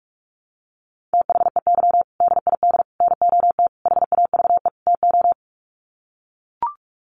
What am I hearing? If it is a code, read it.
Morse code, heard as a keyed tone, about 44 wpm, 700 Hz. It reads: THEY DID NOT HAVE TO
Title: Sets of 5 Words Encore2 45wpm